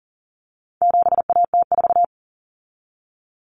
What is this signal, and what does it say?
Morse code, heard as a keyed tone, about 40 wpm, 700 Hz. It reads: 7AT4
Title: Common Cut Numbers - 4-Digit 40wpm